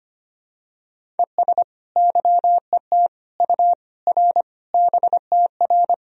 Morse code, heard as a keyed tone, about 25 wpm, 700 Hz. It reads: ES YET U R BTR
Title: Sentences from Top 500 Words and Abbreviations 25wpm